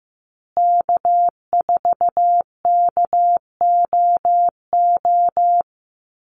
Morse code, heard as a keyed tone, about 15 wpm, 700 Hz. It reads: K4KOO
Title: USA Call Signs Repeated 2x 15wpm